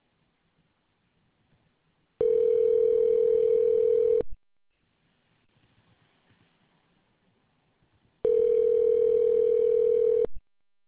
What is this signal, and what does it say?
Ringback tone (North America), 440 + 480 Hz.